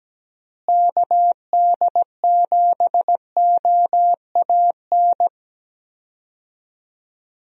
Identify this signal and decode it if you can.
Morse code, heard as a keyed tone, about 17 wpm, 700 Hz. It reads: KD7OAN
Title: USA Call Signs - 6 Characters - Repeated 3x - 17wpm